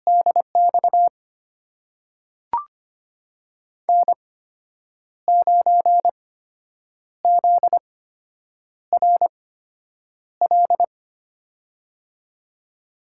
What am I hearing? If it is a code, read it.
Morse code, heard as a keyed tone, about 25 wpm, spaced as for 8 wpm, 700 Hz. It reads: DX N9ZRL
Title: USA Call Signs - 8x Character Spacing - 25wpm